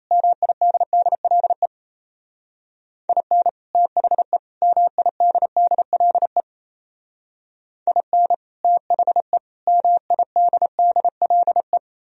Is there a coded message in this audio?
Yes: Morse code from 0.1 to 11.8 s, heard as a keyed tone, about 36 wpm, 700 Hz. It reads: MIDDLE IN THE MIDDLE IN THE MIDDLE